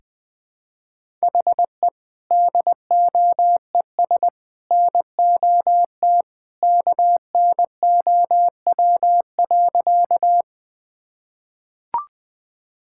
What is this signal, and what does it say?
Morse code, heard as a keyed tone, about 20 wpm, 700 Hz. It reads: HE DOES NOT KNOW.
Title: Sentences with 3-Word Phrases - 20wpm